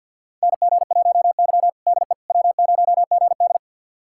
Morse code, heard as a keyed tone, about 48 wpm, 700 Hz. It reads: NG1Y DE W0GD
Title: USA Callsign DE Callsign - 4 Characters - 50wpm